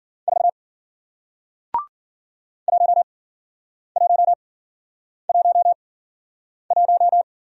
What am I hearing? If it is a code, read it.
Morse code, heard as a keyed tone, about 51 wpm, spaced as for 9 wpm, 700 Hz. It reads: 4 1111